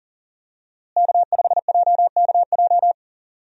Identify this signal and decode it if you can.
Morse code, heard as a keyed tone, about 40 wpm, 700 Hz. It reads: K5JKJ
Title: USA Call Signs - 5 Characters - Repeated 3x - 40wpm